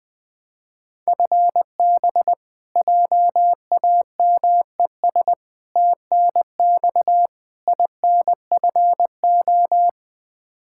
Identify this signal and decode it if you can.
Morse code, heard as a keyed tone, about 21 wpm, 700 Hz. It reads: FB JAMES TNX INFO
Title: Practice QSO - Exchange 3 - More Info - 20wpm